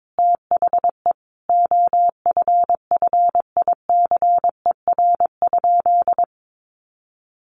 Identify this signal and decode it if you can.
Morse code, heard as a keyed tone, about 22 wpm, 700 Hz. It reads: THE OFFICER?
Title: Sentences from Top 1200 Words 22wpm